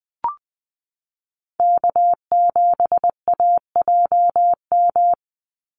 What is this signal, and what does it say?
Morse code, heard as a keyed tone, about 20 wpm, 700 Hz. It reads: K7AJM
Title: Call Signs - 20wpm